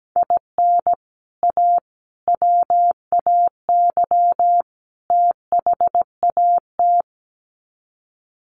Morse code, heard as a keyed tone, about 17 wpm, 700 Hz. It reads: IN A WAY THAT
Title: Sets of 4 Words 17wpm